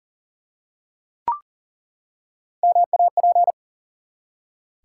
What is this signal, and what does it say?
Morse code, heard as a keyed tone, about 40 wpm, 700 Hz. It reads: MAP
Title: Three Letter Words - 40wpm